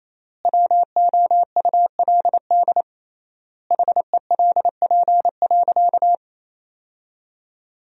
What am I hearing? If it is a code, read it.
Morse code, heard as a keyed tone, about 28 wpm, 700 Hz. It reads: WOULD HELP.
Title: 3x Word-Spacing - Sentences from Top 200 Words - 28wpm